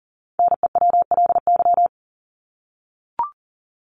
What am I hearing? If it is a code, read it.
Morse code, heard as a keyed tone, about 40 wpm, 700 Hz. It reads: NEWLY
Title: LY Suffix - Top 100 Words with Base Words 40wpm